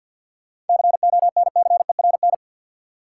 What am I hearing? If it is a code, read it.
Morse code, heard as a keyed tone, about 53 wpm, 700 Hz. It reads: CONCERN
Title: Seven Letter Words - 50wpm